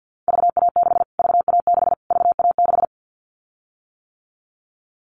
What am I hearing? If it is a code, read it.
Morse code, heard as a keyed tone, about 50 wpm, 700 Hz. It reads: VA6 VA6 VA6
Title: Callsign Prefixes for Top 25 DXCC - Repeated 3x - 50wpm